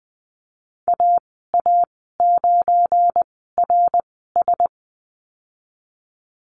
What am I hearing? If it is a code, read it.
Morse code, heard as a keyed tone, about 20 wpm, spaced as for 14 wpm, 700 Hz. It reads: AA9RS
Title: 186 Lesson 30g - Letter b - Call Signs 20wpm